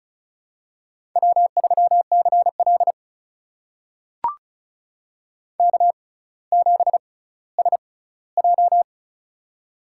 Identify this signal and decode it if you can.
Morse code, heard as a keyed tone, about 36 wpm, spaced as for 18 wpm, 700 Hz. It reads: W3CL K7SJ
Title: USA Call Signs - 4 Characters - 6x Character Spacing - 35wpm